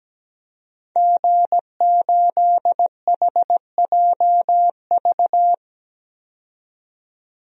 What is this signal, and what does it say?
Morse code, heard as a keyed tone, about 17 wpm, 700 Hz. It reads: G8HJV
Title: DXCC-100 Call Signs - 5 Characters - 17wpm